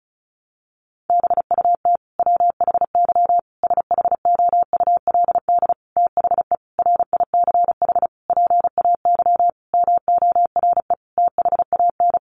Morse code, heard as a keyed tone, about 34 wpm, 700 Hz. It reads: BUT WHY SHOULD THE RICH PAY MORE THAN
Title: Sentences from Top 1100 Words 35wpm